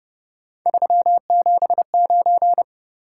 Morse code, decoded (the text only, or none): 379